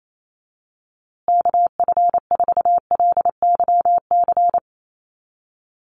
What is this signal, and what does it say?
Morse code, heard as a keyed tone, about 28 wpm, 700 Hz. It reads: KF4LYC
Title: USA Call Signs Repeated 2x 28wpm